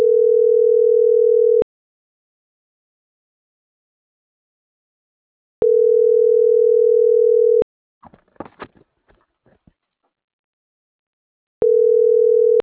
Ringback tone (North America), 440 + 480 Hz.